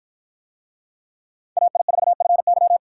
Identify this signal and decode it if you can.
Morse code, heard as a keyed tone, about 53 wpm, 700 Hz. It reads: AI4FY